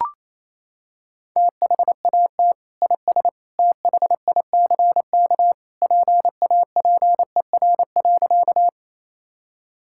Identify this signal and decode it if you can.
Morse code, heard as a keyed tone, about 28 wpm, 700 Hz. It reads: THAT IS THICK PAPER.
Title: Sentences from Top 1600 Words 28wpm